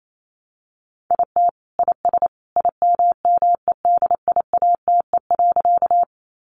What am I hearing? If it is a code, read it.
Morse code, heard as a keyed tone, about 28 wpm, 700 Hz. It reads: IT IS IMMEDIATE.
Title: Sentences from Top 1200 Words 28wpm